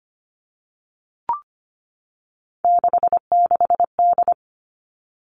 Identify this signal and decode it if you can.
Morse code, heard as a keyed tone, about 25 wpm, 700 Hz. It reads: 66D
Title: All Cut Numbers - 3-Digit 25wpm